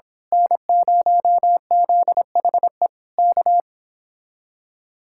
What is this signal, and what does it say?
Morse code, heard as a keyed tone, about 26 wpm, 700 Hz. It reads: N0ZHE K
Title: Speed-Racing Common QSO Phrases - 22wpm